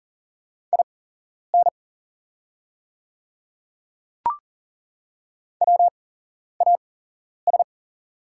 Morse code, heard as a keyed tone, about 40 wpm, spaced as for 11 wpm, 700 Hz. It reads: IN WAS
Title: 8x Character Spacing - Top 100 Words - 40wpm